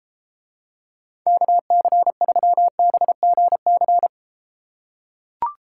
Morse code, heard as a keyed tone, about 32 wpm, 700 Hz. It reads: KC3BGC